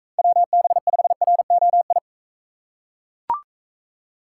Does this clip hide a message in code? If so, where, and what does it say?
Yes: Morse code from 0.2 to 2.0 s, heard as a keyed tone, about 41 wpm, 700 Hz. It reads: WB5ROI